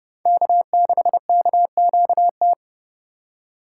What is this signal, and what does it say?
Morse code, heard as a keyed tone, about 31 wpm, 700 Hz. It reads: K6KQT
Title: USA Call Signs - 5 Characters - 2x Character Spacing - 30wpm